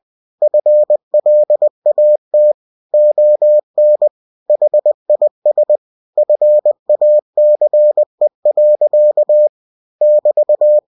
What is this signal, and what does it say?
Morse code, heard as a keyed tone, about 20 wpm, 600 Hz. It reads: FLAT ON HIS FACE. <BT>